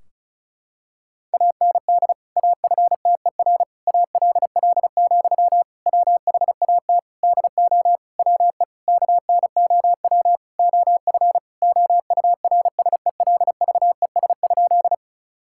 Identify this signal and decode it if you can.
Morse code, heard as a keyed tone, about 34 wpm, 700 Hz. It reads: AND AFTER ALL, WHAT DO WE KNOW OF OURSELVES?
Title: Sentences from Top 1600 Words 35wpm